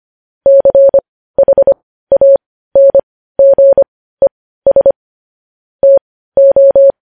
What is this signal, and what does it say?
Morse code, heard as a keyed tone, about 25 wpm, spaced as for 15 wpm, 550 Hz. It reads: CHANGES TO
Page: news headlines in morse code at 15 wpm